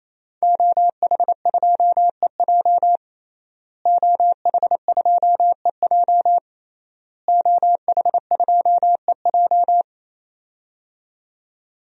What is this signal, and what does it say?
Morse code, heard as a keyed tone, about 28 wpm, 700 Hz. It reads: OH2EJ OH2EJ OH2EJ
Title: DXCC-100 Call Signs - 5 Characters - Repeated 3x - 28wpm